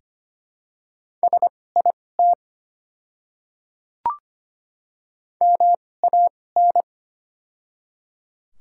Morse code, heard as a keyed tone, about 25 wpm, spaced as for 18 wpm, 700 Hz. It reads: SIT MAN
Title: 2x Character Spacing - Top 500 Words - 25wpm